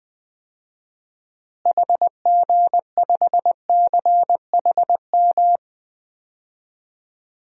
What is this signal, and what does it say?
Morse code, heard as a keyed tone, about 20 wpm, 700 Hz. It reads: HG5CHM
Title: DXCC-100 Call Signs - 6 Characters - Repeated 3x - 20wpm